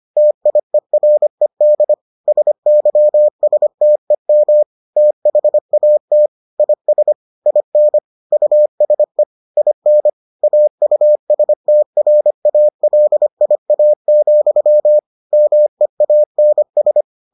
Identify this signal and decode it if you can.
Morse code, heard as a keyed tone, about 25 wpm, 600 Hz. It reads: TIERED SYSTEM THAT IS IN USE IN AUSTRALIA, MEANS